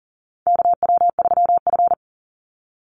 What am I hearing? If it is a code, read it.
Morse code, heard as a keyed tone, about 40 wpm, 700 Hz. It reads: KW3F
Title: Call Signs Repeated 2x - 40wpm